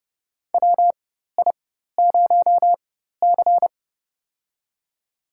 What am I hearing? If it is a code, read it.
Morse code, heard as a keyed tone, about 30 wpm, spaced as for 14 wpm, 700 Hz. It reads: WI0C
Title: USA Call Signs - 4 Characters - 4x Character Spacing - 30wpm